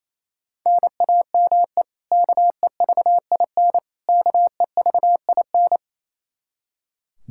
Morse code, heard as a keyed tone, about 28 wpm, 700 Hz. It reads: NAME KEVIN KEVIN
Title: Practice QSO - Exchange 1 - Beginning - 28wpm